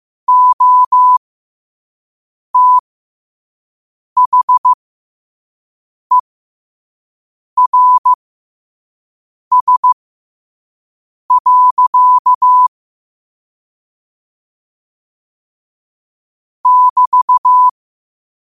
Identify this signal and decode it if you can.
Morse code, heard as a keyed tone, about 15 wpm, spaced as for 5 wpm, 1000 Hz. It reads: OTHERS. <BT>
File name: qotdfiles / 5wpm